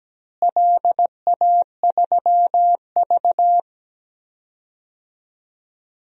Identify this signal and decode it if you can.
Morse code, heard as a keyed tone, about 17 wpm, 700 Hz. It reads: LA3V
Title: DXCC-100 Call Signs - 4 Characters - Repeated 2x - 17wpm